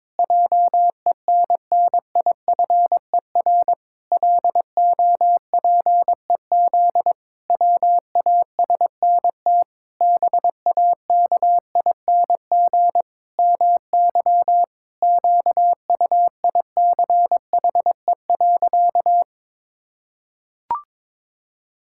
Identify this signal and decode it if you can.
Morse code, heard as a keyed tone, about 22 wpm, 700 Hz. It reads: JENNIFER LOPEZ WASNT BAKING MY QUICHE.
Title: Pangram Challenge v2 - 22wpm